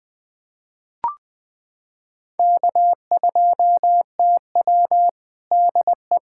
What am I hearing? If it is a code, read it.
Morse code, heard as a keyed tone, about 20 wpm, 700 Hz. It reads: K2TW DE
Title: Practice QSO - Exchange 4 - Ending - 20wpm